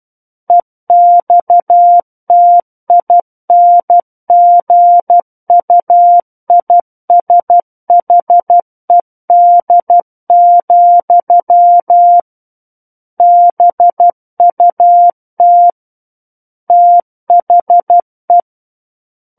Morse code, heard as a keyed tone, about 12 wpm, 700 Hz. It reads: EXTINGUISHED, BUT THE
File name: Book One / 03-Chapter 3 - 12 WPM